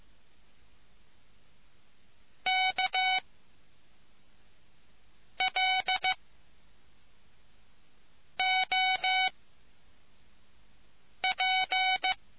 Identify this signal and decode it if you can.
Morse code, heard as a keyed tone, about 15 wpm, 750 Hz. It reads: K L O P